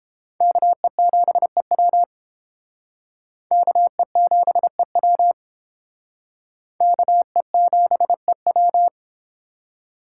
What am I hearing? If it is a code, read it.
Morse code, heard as a keyed tone, about 31 wpm, 700 Hz. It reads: KE7EW KE7EW KE7EW